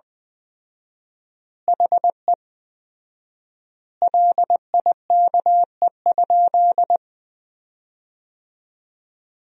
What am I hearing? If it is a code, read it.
Morse code, heard as a keyed tone, about 20 wpm, 700 Hz. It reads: HE LIKE?
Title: 4x Word-Spacing - Sentences from Top 100 Words - 20wpm